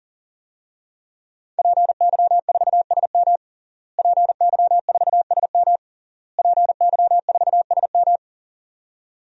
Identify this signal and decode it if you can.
Morse code, heard as a keyed tone, about 40 wpm, 700 Hz. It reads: PY4SM PY4SM PY4SM
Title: DXCC-100 Call Signs - 5 Characters - Repeated 3x - 40wpm